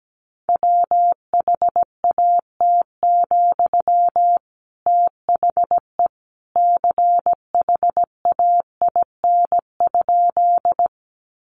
Morse code, heard as a keyed tone, about 17 wpm, 700 Hz. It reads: WHAT, THE CHAIN?
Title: Sentences from Top 1500 Words 17wpm